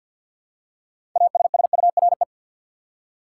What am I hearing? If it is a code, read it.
Morse code, heard as a keyed tone, about 48 wpm, 700 Hz. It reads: ASSURE
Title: Six Letter Words - 50wpm